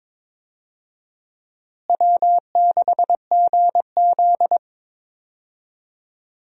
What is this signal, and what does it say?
Morse code, heard as a keyed tone, about 22 wpm, 700 Hz. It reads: W6GZ
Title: USA Call Signs - 4 Characters - Repeated 2x - 22wpm